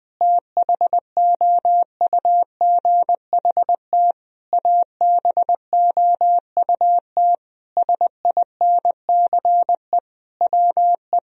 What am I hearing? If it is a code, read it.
Morse code, heard as a keyed tone, about 20 wpm, 700 Hz. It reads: THOUGHT ABOUT SINCE WE
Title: Sentences from Top 1000 Words - Review Part 1 - 20wpm